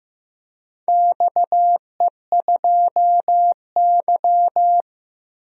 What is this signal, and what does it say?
Morse code, heard as a keyed tone, about 15 wpm, 700 Hz. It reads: XE2Y
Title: DXCC-100 Call Signs - 4 Characters - Repeated 2x - 15wpm